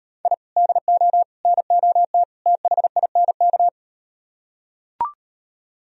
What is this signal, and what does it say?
Morse code, heard as a keyed tone, about 38 wpm, 700 Hz. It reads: I DO NOT THINK